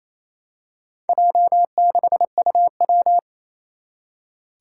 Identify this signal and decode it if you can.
Morse code, heard as a keyed tone, about 28 wpm, 700 Hz. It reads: J6UW
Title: Four Challenging Letter-Number Combinations 28wpm